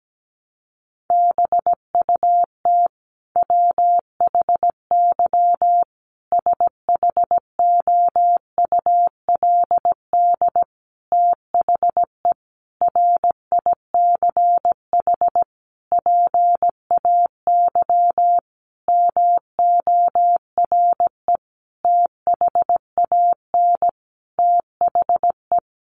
Morse code, heard as a keyed tone, about 17 wpm, 700 Hz. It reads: BUT WHY SHOULD THE RICH PAY MORE THAN THE
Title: Sentences from Top 1100 Words 17wpm